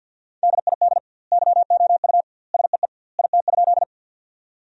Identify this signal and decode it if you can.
Morse code, heard as a keyed tone, about 50 wpm, 700 Hz. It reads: DID YOU SEE IT?
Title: 2x Word-Spacing - Sentences from Top 100 Words - 50wpm